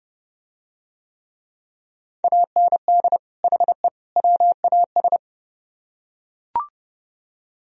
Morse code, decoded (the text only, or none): AND HE WAS